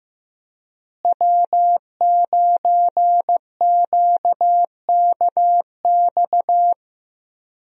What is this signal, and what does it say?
Morse code, heard as a keyed tone, about 15 wpm, 700 Hz. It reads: W9QKX
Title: USA Call Signs - 5 Characters - Repeated 2x - 15wpm